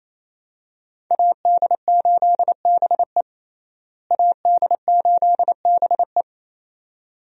Morse code, heard as a keyed tone, about 28 wpm, 700 Hz. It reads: AD8BE AD8BE
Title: USA Call Signs - 5 Characters - Repeated 2x - 28wpm